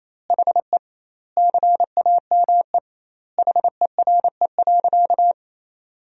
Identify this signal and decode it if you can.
Morse code, heard as a keyed tone, about 28 wpm, 700 Hz. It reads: HE CAME HERE.
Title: 2x Word-Spacing - Sentences from Top 500 Words - 28wpm